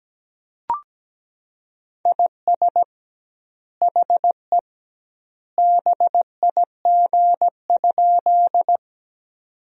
Morse code, heard as a keyed tone, about 17 wpm, 700 Hz. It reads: IS HE BIG?
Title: 2x Word-Spacing - Sentences from Top 200 Words - 17wpm